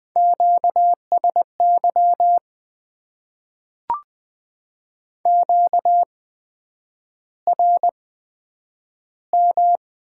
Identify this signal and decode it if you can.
Morse code, heard as a keyed tone, about 20 wpm, spaced as for 9 wpm, 700 Hz. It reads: QSY QRM
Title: Top 90 QSO Elements - 8x Character Spacing - 20wpm